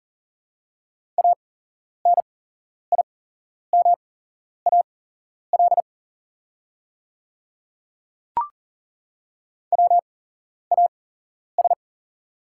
Morse code, heard as a keyed tone, about 39 wpm, spaced as for 11 wpm, 690 Hz. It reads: ANIMAL WAS
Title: 8x Character Spacing - Top 200 Words - 40wpm